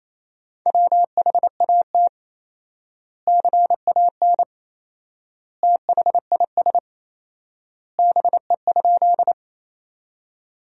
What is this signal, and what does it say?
Morse code, heard as a keyed tone, about 28 wpm, 700 Hz. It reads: WHAT CAN THIS BE?
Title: 4x Word-Spacing - Sentences from Top 100 Words - 28wpm